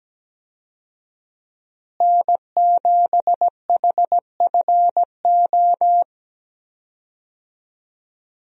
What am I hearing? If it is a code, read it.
Morse code, heard as a keyed tone, about 17 wpm, 700 Hz. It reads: N7HFO
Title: USA Call Signs - 5 Characters - Repeated 3x - 17wpm